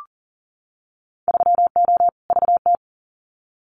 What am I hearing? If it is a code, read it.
Morse code, heard as a keyed tone, about 40 wpm, 700 Hz. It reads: 3O VT
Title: Winter Field Day Exchanges Encore 40wpm